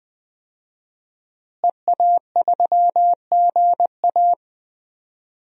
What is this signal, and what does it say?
Morse code, heard as a keyed tone, about 20 wpm, 700 Hz. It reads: EA3GA
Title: DXCC-100 Call Signs - 5 Characters - Repeated 3x - 20wpm